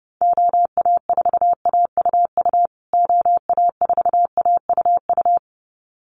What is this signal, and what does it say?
Morse code, heard as a keyed tone, about 28 wpm, 700 Hz. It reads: OA4AUU OA4AUU
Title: USA Call Signs Repeated 2x 30wpm